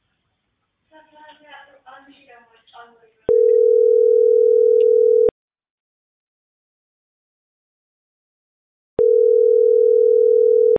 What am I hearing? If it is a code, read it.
Ringback tone (North America), 440 + 480 Hz, with speech.
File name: outgoing recordings